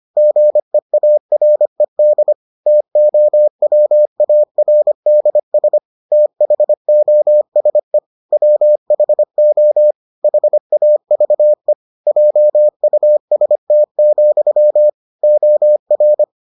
Morse code, heard as a keyed tone, about 25 wpm, 600 Hz. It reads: GEARED TOWARDS THOSE WHO HAVE JUST, OR